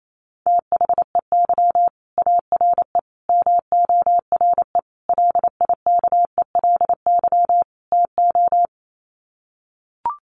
Morse code, heard as a keyed tone, about 28 wpm, 700 Hz. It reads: THEY ARE MORE LIKELY TO
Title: Sets of 5 Words Encore2 28wpm